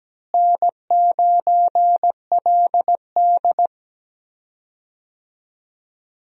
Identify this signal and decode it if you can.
Morse code, heard as a keyed tone, about 17 wpm, 700 Hz. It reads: N9LD
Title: USA Call Signs - 4 Characters - Repeated 3x - 17wpm